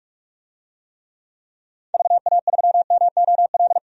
Morse code, heard as a keyed tone, about 46 wpm, 700 Hz. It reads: VA3MOL
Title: DXCC-100 Call Signs - 6 Characters - 45wpm